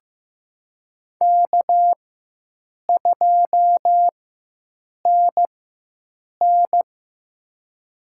Morse code, heard as a keyed tone, about 15 wpm, spaced as for 7 wpm, 700 Hz. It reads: K2NN
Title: USA Call Signs - 4 Characters - 4x Character Spacing - 15wpm